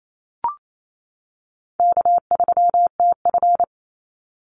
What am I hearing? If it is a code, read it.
Morse code, heard as a keyed tone, about 28 wpm, 700 Hz. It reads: K3TF
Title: Easy Call Signs - 28wpm